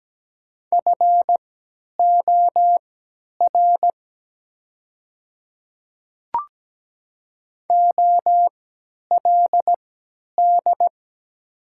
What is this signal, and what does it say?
Morse code, heard as a keyed tone, about 17 wpm, spaced as for 10 wpm, 700 Hz. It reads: FOR OLD
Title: 3x Character Spacing - Top 200 Words - 17wpm